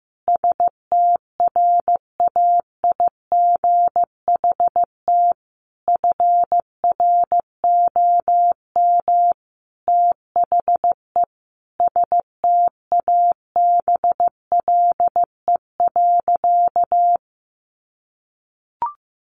Morse code, heard as a keyed tone, about 15 wpm, 700 Hz. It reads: STRAIGHT FROM THE STABLE.
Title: Sentences from Top 1500 Words 15wpm